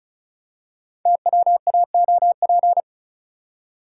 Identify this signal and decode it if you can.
Morse code, heard as a keyed tone, about 36 wpm, 700 Hz. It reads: TWAOP
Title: Code Groups - 5 Letters - 35wpm